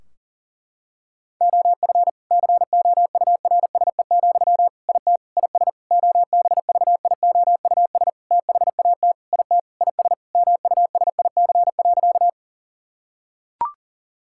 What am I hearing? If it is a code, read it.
Morse code, heard as a keyed tone, about 40 wpm, 700 Hz. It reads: OF COURSE, IT IS OBVIOUS THAT IT IS MUSIC.
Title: Sentences from Top 1600 Words 40wpm